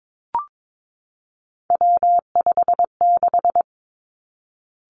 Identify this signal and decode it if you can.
Morse code, heard as a keyed tone, about 22 wpm, 700 Hz. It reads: W56
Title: Three Challenging Letter-Number Combinations 22wpm